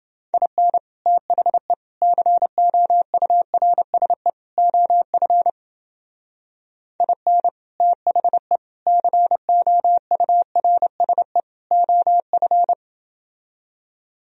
Morse code, heard as a keyed tone, about 29 wpm, 700 Hz. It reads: IN THE COURSE OF IN THE COURSE OF